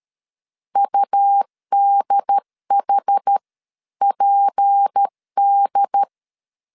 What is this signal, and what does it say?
Morse code, heard as a keyed tone, about 13 wpm, spaced as for 11 wpm, 800 Hz. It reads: UDHPD
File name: CW / DET / 1 6 1500C S C 12wpm 3533